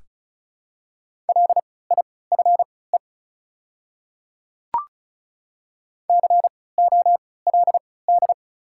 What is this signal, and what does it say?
Morse code, heard as a keyed tone, about 36 wpm, spaced as for 20 wpm, 700 Hz. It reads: LIFE COLD
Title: 3x Character Spacing - Top 500 Words - 35wpm